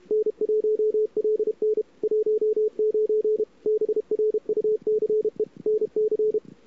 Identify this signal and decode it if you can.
Morse code, heard as a keyed tone, about 33 wpm, 400 Hz. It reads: N1LN 19 BRUCE NC
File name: CWSprint / Feb2007 / 2007-feb-cw-sprint-k5tr-op-n5rz -00:00